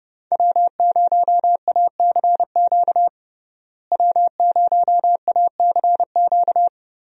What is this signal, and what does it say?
Morse code, heard as a keyed tone, about 30 wpm, 700 Hz. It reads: W0ACQ W0ACQ
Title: USA Call Signs - 5 Characters - Repeated 3x - 30wpm